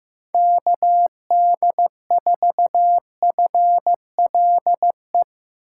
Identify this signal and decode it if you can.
Morse code, heard as a keyed tone, about 16 wpm, 700 Hz. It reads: KD4FLE
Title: USA Call Signs - 6 Characters - 4x Character Spacing - 15wpm